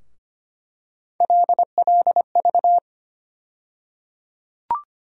Morse code, heard as a keyed tone, about 25 wpm, 700 Hz. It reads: LLV
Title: USA Call Sign Postfixes - 3 Letters 25wpm